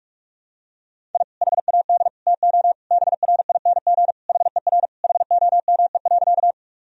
Morse code, heard as a keyed tone, about 46 wpm, 700 Hz. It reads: I HAD TO BRING HER HOME.
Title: Sentences from Top 1000 Words - Review Part 3 - 45wpm